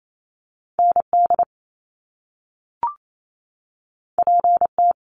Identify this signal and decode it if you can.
Morse code, heard as a keyed tone, about 28 wpm, 700 Hz. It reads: ND PT
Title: Top Two Letter Combinations in Words 28wpm